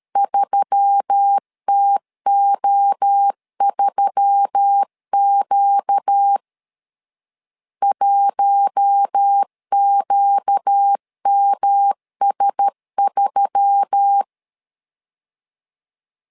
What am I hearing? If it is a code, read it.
Morse code, heard as a keyed tone, about 13 wpm, 800 Hz. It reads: 3TO3Q 1QMS3
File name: LEZIONI / 4 Q1S3 12wpm 2023